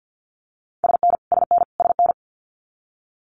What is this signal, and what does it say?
Morse code, heard as a keyed tone, about 50 wpm, 700 Hz. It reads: SN SN SN
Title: Abbreviations Repeated 3x - 50wpm